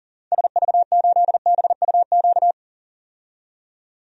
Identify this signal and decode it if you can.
Morse code, heard as a keyed tone, about 40 wpm, 700 Hz. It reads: SV8BUQ
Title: DXCC-100 Call Signs - 6 Characters - 40wpm